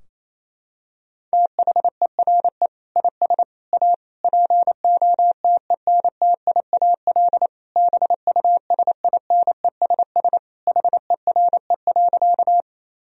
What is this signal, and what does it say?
Morse code, heard as a keyed tone, about 28 wpm, 700 Hz. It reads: THERE IS A POTENTIAL BUSINESS HERE.
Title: Sentences from Top 1000 Words - Review Part 4 - 28wpm